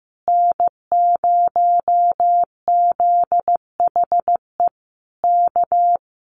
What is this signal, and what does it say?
Morse code, heard as a keyed tone, about 15 wpm, 700 Hz. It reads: N0ZHE K
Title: Common QSO Phrases Encore 15wpm